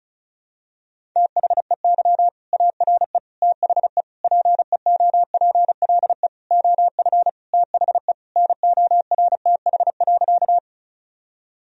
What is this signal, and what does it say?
Morse code, heard as a keyed tone, about 36 wpm, 700 Hz. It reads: THEY ARE THE PEOPLE OF THE NORTH.
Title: Sentences from Top 1000 Words - Review Part 1 - 35wpm